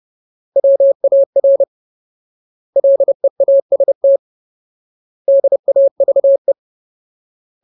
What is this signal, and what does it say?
Morse code, heard as a keyed tone, about 30 wpm, 550 Hz. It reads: WAR LEAST DAVE
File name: Ultimate Word List / 30WPM Ultimate